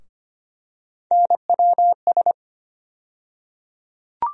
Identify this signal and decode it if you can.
Morse code, heard as a keyed tone, about 25 wpm, 700 Hz. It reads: NWS
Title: USA Call Sign Postfixes - 3 Letters 25wpm